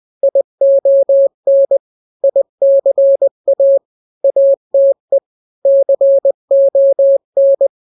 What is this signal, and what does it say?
Morse code, heard as a keyed tone, about 20 wpm, 550 Hz. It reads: ION ICA ATE CON
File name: n-grams / 3-grams 20 18 WPM